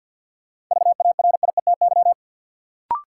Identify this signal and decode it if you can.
Morse code, heard as a keyed tone, about 47 wpm, 700 Hz. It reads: VARIETY
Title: Sets of 3 Words - New Word Drill - 50wpm